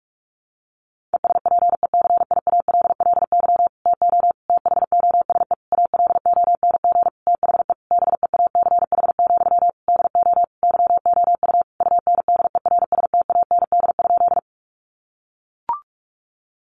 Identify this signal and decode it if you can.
Morse code, heard as a keyed tone, about 44 wpm, 700 Hz. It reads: ESPECIALLY TO THOSE ALONG THE BEACH, DO YOU UNDERSTAND?
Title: Sentences from Top 1500 Words 45wpm